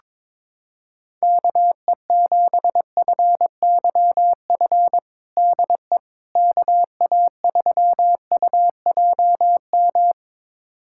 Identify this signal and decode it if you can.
Morse code, heard as a keyed tone, about 22 wpm, 700 Hz. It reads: KE7FYF DE KA3UJM
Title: USA Callsign DE Callsign - 6 Characters - 22wpm